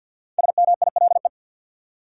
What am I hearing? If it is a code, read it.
Morse code, heard as a keyed tone, about 49 wpm, 700 Hz. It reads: SMILE
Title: Five Letter Words - 50wpm